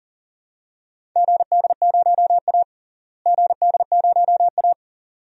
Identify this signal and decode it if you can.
Morse code, heard as a keyed tone, about 40 wpm, 700 Hz. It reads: GD0A GD0A
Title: DXCC-100 Call Signs - 4 Characters - Repeated 2x - 40wpm